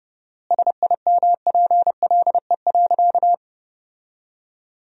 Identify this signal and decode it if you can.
Morse code, heard as a keyed tone, about 31 wpm, 700 Hz. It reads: SIMPLE.
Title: 4x Word-Spacing - Sentences from Top 500 Words - 30wpm